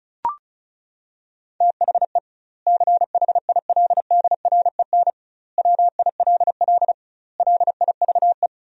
Morse code, heard as a keyed tone, about 36 wpm, 700 Hz. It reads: THE CHILDREN WILL LIVE
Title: 2x Word-Spacing - Sentences from Top 300 Words - 35wpm